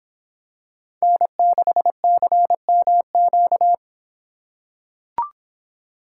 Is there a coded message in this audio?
Yes: Morse code from 1.0 to 3.7 s, heard as a keyed tone, about 25 wpm, 700 Hz. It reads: N6CMQ